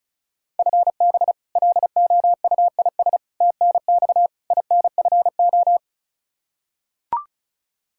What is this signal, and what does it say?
Morse code, heard as a keyed tone, about 36 wpm, 700 Hz. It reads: FB LOUIS TNX INFO
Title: K1USN SST QSO Practice 35wpm